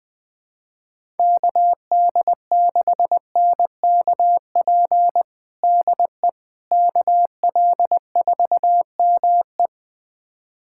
Morse code, heard as a keyed tone, about 21 wpm, 700 Hz. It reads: KD6NKP DE KL4ME
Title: USA Callsign DE Callsign 20wpm